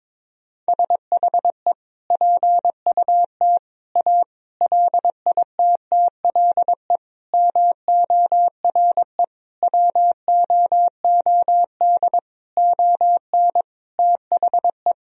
Morse code, heard as a keyed tone, about 22 wpm, 700 Hz. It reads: SHE PUT A LITTLE MORE WOOD ON THE
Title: Sentences from Top 1000 Words - Review Part 2 - 22wpm